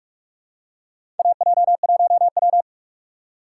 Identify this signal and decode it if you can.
Morse code, heard as a keyed tone, about 42 wpm, 700 Hz. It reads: AJ1W
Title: USA Call Signs - 4 Characters - 6x Character Spacing - 45wpm